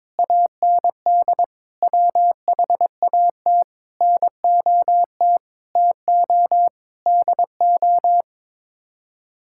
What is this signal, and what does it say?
Morse code, heard as a keyed tone, about 22 wpm, 700 Hz. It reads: AND WHAT NOT TO DO
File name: practice / 22wpm / Sets of 8 Words - 22wpm